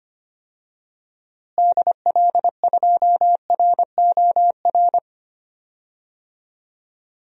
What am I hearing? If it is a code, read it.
Morse code, heard as a keyed tone, about 25 wpm, 700 Hz. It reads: DL2ROR
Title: DXCC-100 Call Signs - 6 Characters - Repeated 2x - 25wpm